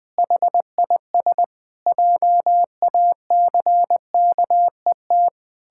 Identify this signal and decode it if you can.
Morse code, heard as a keyed tone, about 21 wpm, 700 Hz. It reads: HIS JACKET
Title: Pangram Challenge v2 - 20wpm